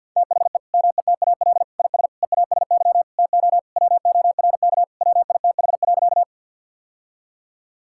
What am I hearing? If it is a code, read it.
Morse code, heard as a keyed tone, about 50 wpm, 700 Hz. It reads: THE METAL IS EASY TO WORK WITH.
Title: Sentences from Top 1000 Words - Review Part 6 - 50wpm